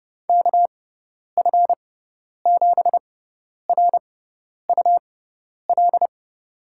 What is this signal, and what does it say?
Morse code, heard as a keyed tone, about 30 wpm, spaced as for 10 wpm, 700 Hz. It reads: KF7RUL
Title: USA Call Signs - 6 Characters - 6x Character Spacing - 30wpm